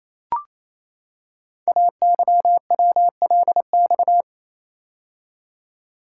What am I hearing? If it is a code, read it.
Morse code, heard as a keyed tone, about 27 wpm, 700 Hz. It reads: AYWLX
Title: Code Groups - 5 Letters - 28wpm